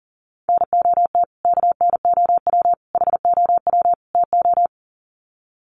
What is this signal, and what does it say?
Morse code, heard as a keyed tone, about 40 wpm, 700 Hz. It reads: NOT KNOW HOW TO
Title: Sets of 4 Words 40wpm